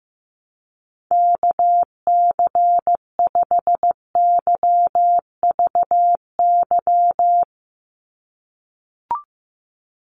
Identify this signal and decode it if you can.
Morse code, heard as a keyed tone, about 15 wpm, 700 Hz. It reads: KC5YVY
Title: USA Call Signs 15wpm